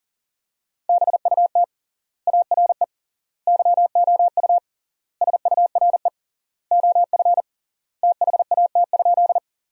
Morse code, heard as a keyed tone, about 40 wpm, 700 Hz. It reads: BUT ARE YOU SURE OF THAT?
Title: 3x Word-Spacing - Sentences from Top 300 Words - 40wpm